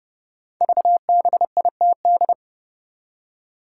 Morse code, heard as a keyed tone, about 30 wpm, 700 Hz. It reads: VBITD
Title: Code Groups - 5 Letters - 30wpm